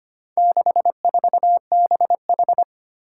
Morse code, heard as a keyed tone, about 25 wpm, 700 Hz. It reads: 64BH
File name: practice / 25wpm / Four Letter-Number Combinations - 25wpm